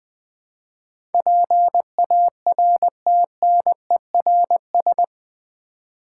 Morse code, heard as a keyed tone, about 20 wpm, 700 Hz. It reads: PARTNERS
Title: Build A Word - 4 Letters At A Time - 11-Letter Words - 20wpm